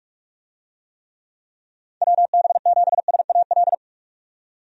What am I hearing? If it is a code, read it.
Morse code, heard as a keyed tone, about 42 wpm, 700 Hz. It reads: WB7SAL
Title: USA Call Signs - 6 Characters - 45wpm